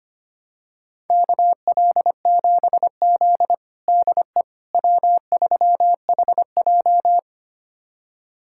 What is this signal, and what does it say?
Morse code, heard as a keyed tone, about 25 wpm, 700 Hz. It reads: KL7Z DE W3HJ
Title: USA Callsign DE Callsign - 4 Characters - 25wpm